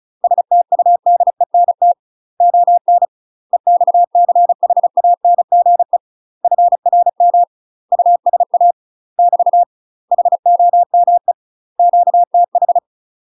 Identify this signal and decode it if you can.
Morse code, heard as a keyed tone, about 34 wpm, 700 Hz. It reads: STUDENT ON EXCHANGE FRM USA <BT> HOME QTH
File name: UK MASTER QSOs TRAINING / 35WPM / UK MASTER QSOs 0015